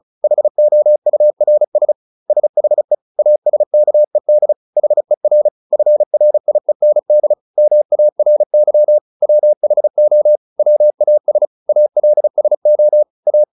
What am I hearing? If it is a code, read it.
Morse code, heard as a keyed tone, about 35 wpm, 600 Hz. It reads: HOURS SHE ASKED HER FRIEND MARY WHO WAS ALSO A
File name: SS303 35